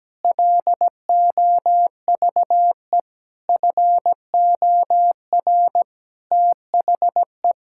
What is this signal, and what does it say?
Morse code, heard as a keyed tone, about 17 wpm, 700 Hz. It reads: LOVE FOR THE
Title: Sentences from Top 1600 Words 17wpm